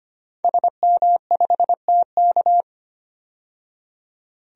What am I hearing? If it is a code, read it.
Morse code, heard as a keyed tone, about 25 wpm, 700 Hz. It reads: SM5TK
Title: DXCC-100 Call Signs - 5 Characters - 25wpm